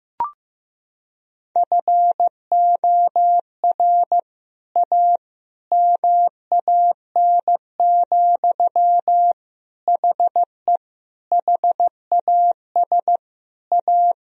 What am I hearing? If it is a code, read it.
Morse code, heard as a keyed tone, about 15 wpm, 700 Hz. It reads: FOR A MAN, HE HAS A
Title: Sentences from Top 1000 Words - Review Part 2 - 15wpm